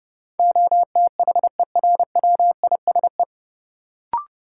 Morse code, heard as a keyed tone, about 31 wpm, 700 Hz. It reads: OTHERWISE
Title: Sentences from Top 1200 Words - New Word Drill - 30wpm